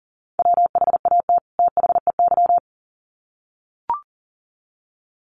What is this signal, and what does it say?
Morse code, heard as a keyed tone, about 40 wpm, 700 Hz. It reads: WHAT THEY
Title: Sets of 2 Words 40wpm